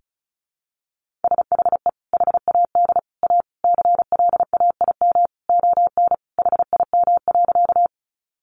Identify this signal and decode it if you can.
Morse code, heard as a keyed tone, about 34 wpm, 700 Hz. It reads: SHE HAD A CLAIM ON HIM.
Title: Sentences from Top 1500 Words 35wpm